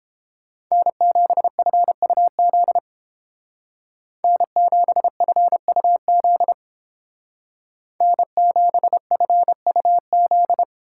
Morse code, heard as a keyed tone, about 31 wpm, 700 Hz. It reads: N7FUZ N7FUZ N7FUZ